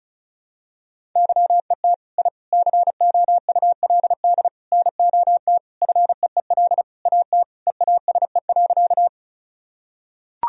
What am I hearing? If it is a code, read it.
Morse code, heard as a keyed tone, about 36 wpm, 700 Hz. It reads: YET I COULD NOT FEEL AT EASE.
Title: Sentences from Top 1000 Words - Review Part 1 - 35wpm